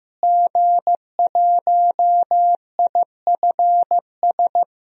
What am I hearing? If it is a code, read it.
Morse code, heard as a keyed tone, about 15 wpm, 700 Hz. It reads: G1IFS
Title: DXCC-100 Call Signs - 5 Characters - Repeated 3x - 15wpm